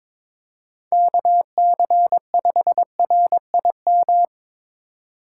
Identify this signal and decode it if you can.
Morse code, heard as a keyed tone, about 22 wpm, 700 Hz. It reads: KC5RIM
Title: USA Call Signs - 6 Characters - 8x Character Spacing - 22wpm